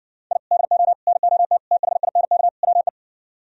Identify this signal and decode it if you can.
Morse code, heard as a keyed tone, about 69 wpm, 700 Hz. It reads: I DO NOT THINK WE